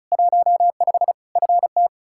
Morse code, heard as a keyed tone, about 36 wpm, 700 Hz. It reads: 15 FT
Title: Practice QSO - Exchange 2 - Info - 35wpm